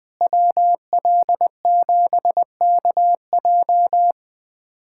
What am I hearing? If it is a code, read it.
Morse code, heard as a keyed tone, about 20 wpm, 700 Hz. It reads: WL7KJ
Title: USA Callsign DE Callsign - 5 Characters - 20wpm